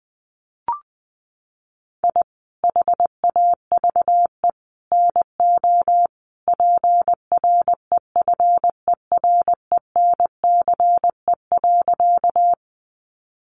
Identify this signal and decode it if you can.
Morse code, heard as a keyed tone, about 20 wpm, 700 Hz. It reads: I HAVE NO PREFERENCE.
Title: Sentences from Top 1500 Words 20wpm